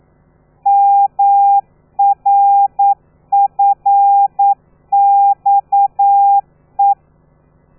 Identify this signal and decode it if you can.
Morse code, heard as a keyed tone, about 9 wpm, 800 Hz. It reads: MRFXE
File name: telegrafia / i2tfj / Lez13 40